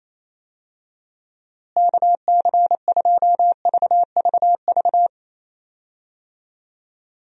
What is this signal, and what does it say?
Morse code, heard as a keyed tone, about 28 wpm, 700 Hz. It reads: KC2VVV
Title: USA Call Signs - 6 Characters - Repeated 2x - 28wpm